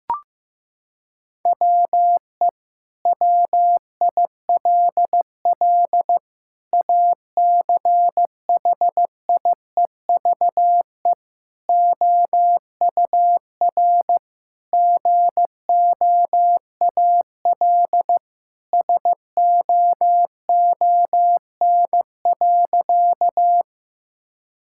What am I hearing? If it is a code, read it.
Morse code, heard as a keyed tone, about 15 wpm, 700 Hz. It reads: WE WILL ACHIEVE OUR GOAL SOON.
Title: Sentences from Top 1000 Words - Review Part 6 - 15wpm